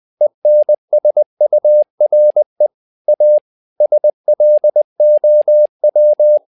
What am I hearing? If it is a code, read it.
Morse code, heard as a keyed tone, about 20 wpm, 600 Hz. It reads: ENSURE A SLOW
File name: learning center / tower trouble / tower trouble 20wpm